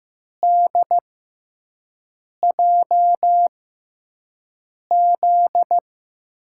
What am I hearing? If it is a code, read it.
Morse code, heard as a keyed tone, about 15 wpm, spaced as for 5 wpm, 700 Hz. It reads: DJZ
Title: USA Call Signs - 6 Characters - 6x Character Spacing - 15wpm